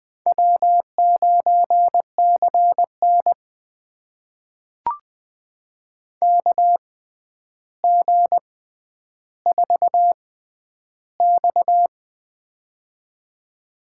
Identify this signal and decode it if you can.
Morse code, heard as a keyed tone, about 20 wpm, spaced as for 10 wpm, 700 Hz. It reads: W9CN KG4X
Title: USA Call Signs - 4 Characters - 6x Character Spacing - 20wpm